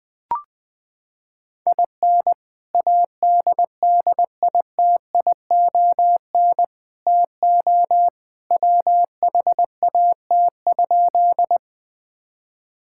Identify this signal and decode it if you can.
Morse code, heard as a keyed tone, about 21 wpm, 700 Hz. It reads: IN ADDITION TO WHAT?
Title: Sentences from Top 1000 Words - Review Part 6 - 20wpm